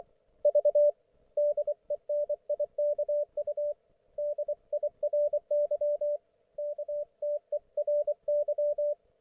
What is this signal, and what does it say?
Morse code, heard as a keyed tone, about 24 wpm, 590 Hz, with radio noise. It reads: V DENIKU DIRY KTERY
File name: Archiv / CUC212 / ok1fhi cul212